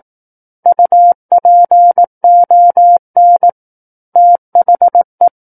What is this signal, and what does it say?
Morse code, heard as a keyed tone, about 18 wpm, 700 Hz. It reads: UPON THE
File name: Book One / 09-Chapter 9 - 18 WPM